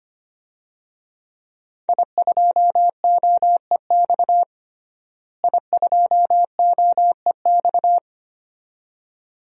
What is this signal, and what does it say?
Morse code, heard as a keyed tone, about 25 wpm, 700 Hz. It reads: I2OEX I2OEX
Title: DXCC-100 Call Signs - 5 Characters - Repeated 2x - 25wpm